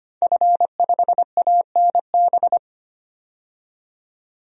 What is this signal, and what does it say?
Morse code, heard as a keyed tone, about 25 wpm, 700 Hz. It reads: F5ANB
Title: DXCC-100 Call Signs - 5 Characters - 25wpm